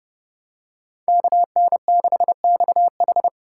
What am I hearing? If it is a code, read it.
Morse code, heard as a keyed tone, about 29 wpm, 700 Hz. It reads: KN6XH